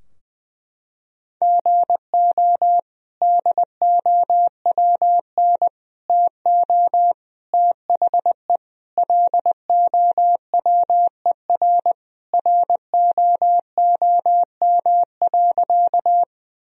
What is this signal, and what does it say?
Morse code, heard as a keyed tone, about 20 wpm, 700 Hz. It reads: GO DOWN TO THE LOWER ROOM.
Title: Sentences from Top 1000 Words - Review Part 4 - 20wpm